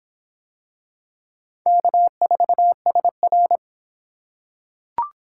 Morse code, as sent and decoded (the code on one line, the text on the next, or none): -.- ....- ... .-.
K4SR